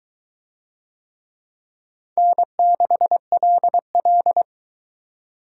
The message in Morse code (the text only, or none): N6LL